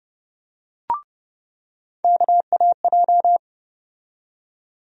Morse code, heard as a keyed tone, about 31 wpm, 700 Hz. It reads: KAJ
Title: USA Call Sign Postfixes - 3 Letters 30wpm